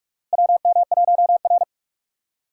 Morse code, heard as a keyed tone, about 46 wpm, 700 Hz. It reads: WM1R
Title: USA Call Signs - 4 Characters - 3x Character Spacing - 45wpm